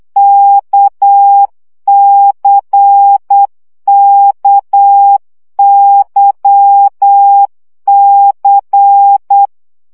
Morse code, heard as a keyed tone, about 8 wpm, 800 Hz. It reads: KCKYC
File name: CW gyakorlas k c y 7 baud